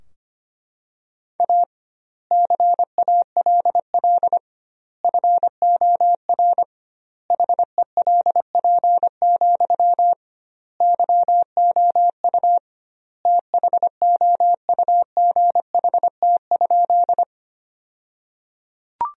Morse code, heard as a keyed tone, about 25 wpm, 700 Hz. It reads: A CALL FOR HELP, YOU THOUGHT?
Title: 2x Word-Spacing - Sentences from Top 300 Words - 25wpm